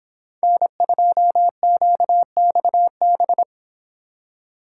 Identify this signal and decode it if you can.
Morse code, heard as a keyed tone, about 26 wpm, 700 Hz. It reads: N2QXB